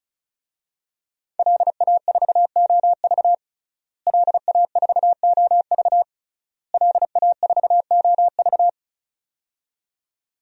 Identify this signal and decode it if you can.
Morse code, heard as a keyed tone, about 36 wpm, 700 Hz. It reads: LA4OV LA4OV LA4OV
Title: DXCC-100 Call Signs - 5 Characters - Repeated 3x - 35wpm